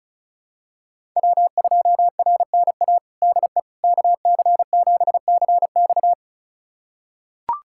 Morse code, heard as a keyed tone, about 34 wpm, 700 Hz. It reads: W2RNA DE KC7CX
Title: USA Callsign DE Callsign - 5 Characters - 35wpm